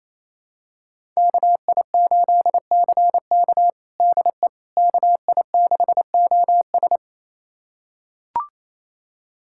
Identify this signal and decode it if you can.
Morse code, heard as a keyed tone, about 28 wpm, 700 Hz. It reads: KI8CK DE KI6OS
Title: USA Callsign DE Callsign - 5 Characters - 28wpm